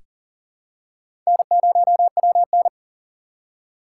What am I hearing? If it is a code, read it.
Morse code, heard as a keyed tone, about 40 wpm, 700 Hz. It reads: N0WN
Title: USA Call Signs - 6x Character Spacing - 40wpm